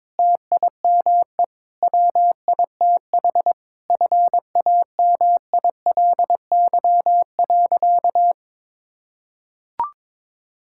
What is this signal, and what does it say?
Morse code, heard as a keyed tone, about 22 wpm, 700 Hz. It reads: TIME WITH FAMILY.
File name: Difficult Sentences with Hints - 22wpm